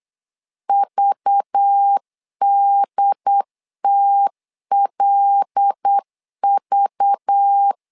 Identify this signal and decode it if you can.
Morse code, heard as a keyed tone, about 8 wpm, 800 Hz. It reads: VDTLV
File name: LEZIONI / 5 LBVC / 5 LBV 8wpm 2837